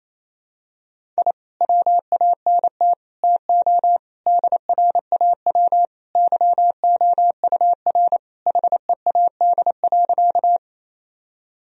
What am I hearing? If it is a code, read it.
Morse code, heard as a keyed tone, about 28 wpm, 700 Hz. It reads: I WANT TO DRAW YOUR HEAD.
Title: Sentences from Top 1000 Words - Review Part 1 - 28wpm